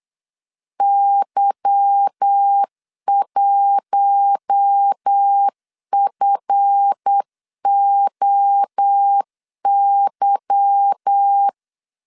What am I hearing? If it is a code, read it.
Morse code, heard as a keyed tone, about 8 wpm, 800 Hz. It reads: Y1FOY
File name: LEZIONI / 8 YJ45 / 8 Y 8wpm 3245